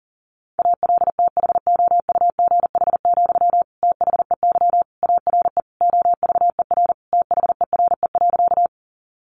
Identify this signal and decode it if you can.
Morse code, heard as a keyed tone, about 40 wpm, 700 Hz. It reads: ALTHOUGH, THEY ARE OVER THERE.
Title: Sentences from Top 600 Words 40wpm